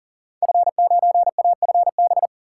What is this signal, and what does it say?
Morse code, heard as a keyed tone, about 39 wpm, 700 Hz. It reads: F9AFB
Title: Code Groups - 5 Letter:Numbers - 40wpm